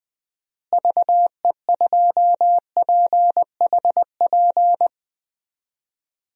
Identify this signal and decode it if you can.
Morse code, heard as a keyed tone, about 20 wpm, 700 Hz. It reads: VE2PHP
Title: DXCC-100 Call Signs - 6 Characters - Repeated 2x - 20wpm